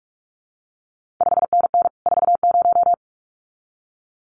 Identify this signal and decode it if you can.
Morse code, heard as a keyed tone, about 46 wpm, 700 Hz. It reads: 5NN 40
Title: CQ WW DX Contest Exchange Encore - 45wpm